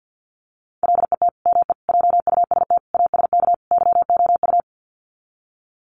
Morse code, heard as a keyed tone, about 48 wpm, 700 Hz. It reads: LET ME JUST ASK YOU
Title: Sets of 5 Words Encore2 50wpm